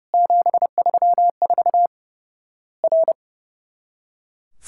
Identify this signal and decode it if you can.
Morse code, heard as a keyed tone, about 30 wpm, 700 Hz. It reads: 734 R
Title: Worked All Europe DX Contest - QTC Exchange - 30wpm